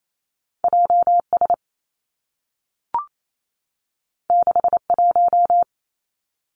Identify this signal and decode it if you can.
Morse code, heard as a keyed tone, about 28 wpm, 700 Hz. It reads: J S 6 1
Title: Two Challenging Letter-Number Combinations 28wpm